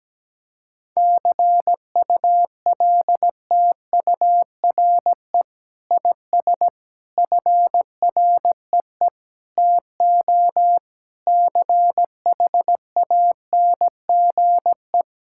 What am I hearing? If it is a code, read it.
Morse code, heard as a keyed tone, about 17 wpm, 700 Hz. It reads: CULTURE IS FREE TO CHANGE
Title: Sentences from Top 1000 Words - Review Part 6 - 17wpm